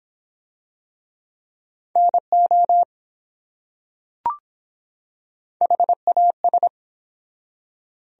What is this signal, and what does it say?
Morse code, heard as a keyed tone, about 23 wpm, 700 Hz. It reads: NO HAS